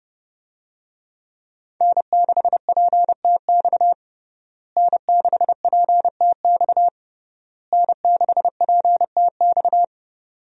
Morse code, heard as a keyed tone, about 30 wpm, 700 Hz. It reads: N6PTX N6PTX N6PTX
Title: USA Call Signs - 5 Characters - Repeated 3x - 30wpm